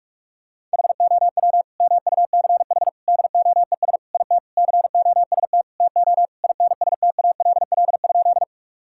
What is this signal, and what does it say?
Morse code, heard as a keyed tone, about 46 wpm, 700 Hz. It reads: HOW MUCH DOES IT COST TO INSTALL?
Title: Sentences from Top 1600 Words 45wpm